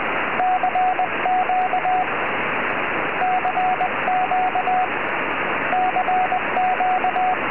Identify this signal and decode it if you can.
Morse code, heard as a keyed tone, about 20 wpm, 740 Hz, with radio noise. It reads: CQ CQ CQ
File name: oh6pa 4m